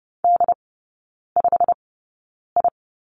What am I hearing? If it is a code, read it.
Morse code, heard as a keyed tone, about 30 wpm, spaced as for 9 wpm, 700 Hz. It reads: D5I
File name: Lesson 16e - Letter c - Three Characters 30wpm